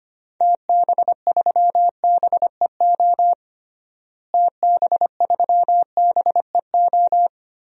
Morse code, heard as a keyed tone, about 25 wpm, 700 Hz. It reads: TB3BEO TB3BEO
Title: DXCC-100 Call Signs - 6 Characters - Repeated 2x - 25wpm